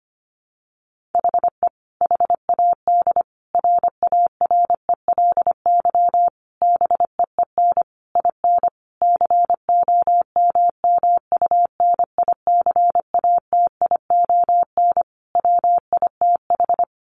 Morse code, heard as a keyed tone, about 25 wpm, 700 Hz. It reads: HE HAD RARELY BEEN IN COMMUNICATION WITH
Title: Sentences from Top 1400 Words 25wpm